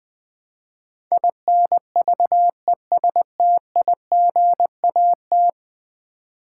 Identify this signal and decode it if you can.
Morse code, heard as a keyed tone, about 21 wpm, 700 Hz. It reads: INVESTIGAT
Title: Build A Word - 5 Letters At A Time - 12-Letter Words - 20wpm